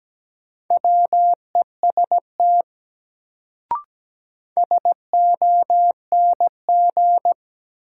Morse code, heard as a keyed tone, about 17 wpm, 700 Hz. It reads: WEST SONG
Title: Sentences from Top 400 Words - New Word Drill - 17wpm